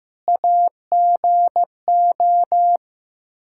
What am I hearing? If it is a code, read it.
Morse code, heard as a keyed tone, about 15 wpm, 700 Hz. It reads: AGO
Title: Practice QSO - Exchange 3 - More Info - 15wpm